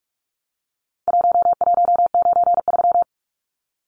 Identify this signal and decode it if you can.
Morse code, heard as a keyed tone, about 44 wpm, 700 Hz. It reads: 1193
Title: Numbers - Contest Style - 4-Digit 45wpm